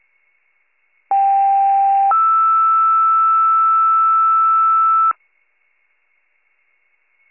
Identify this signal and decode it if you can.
Two-tone page, heard as a one-second tone, then a three-second tone, about 790 then 1320 Hz. It is heard from 1.1 to 5.1 s.